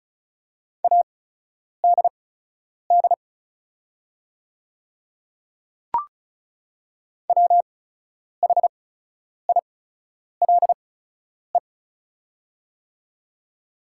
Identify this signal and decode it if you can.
Morse code, heard as a keyed tone, about 34 wpm, spaced as for 10 wpm, 690 Hz. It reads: ADD WHILE
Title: 8x Character Spacing - Top 500 Words - 35wpm